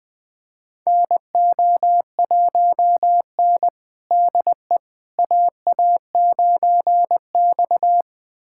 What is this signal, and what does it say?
Morse code, heard as a keyed tone, about 20 wpm, 700 Hz. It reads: NO1N DE AA9X
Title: USA Callsign DE Callsign - 4 Characters - 20wpm